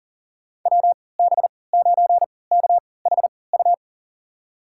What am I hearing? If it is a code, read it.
Morse code, heard as a keyed tone, about 40 wpm, spaced as for 23 wpm, 700 Hz. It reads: WB9KHU
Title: USA Call Signs - 6 Characters - 3x Character Spacing - 40wpm